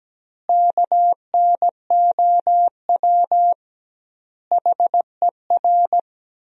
Morse code, heard as a keyed tone, about 17 wpm, 700 Hz. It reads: KNOW HER
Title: 2x Word-Spacing - Sentences from Top 400 Words - 17wpm